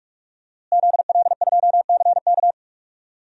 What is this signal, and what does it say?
Morse code, heard as a keyed tone, about 46 wpm, 700 Hz. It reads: ZL2CK
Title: DXCC-100 Call Signs - 5 Characters - Repeated 3x - 45wpm